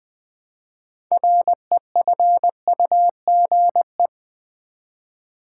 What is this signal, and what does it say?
Morse code, heard as a keyed tone, about 20 wpm, 700 Hz. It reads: REFUGE
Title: Build A Word - 2 Letters At A Time - 6-Letter Words - 20wpm